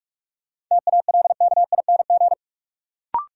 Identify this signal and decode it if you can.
Morse code, heard as a keyed tone, about 44 wpm, 700 Hz. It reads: TALKING
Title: Sets of 2 Words - New Word Drill - 45wpm